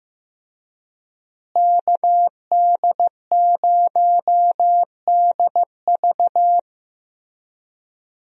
Morse code, heard as a keyed tone, about 15 wpm, 700 Hz. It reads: KD0DV
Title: USA Call Signs - 5 Characters - Repeated 3x - 15wpm